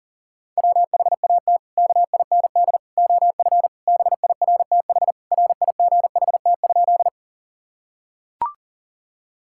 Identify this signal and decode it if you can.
Morse code, heard as a keyed tone, about 40 wpm, 700 Hz. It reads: WHAT KIND OF BIRTH RIGHT?
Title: Sentences from Top 1000 Words - Review Part 5 - 40wpm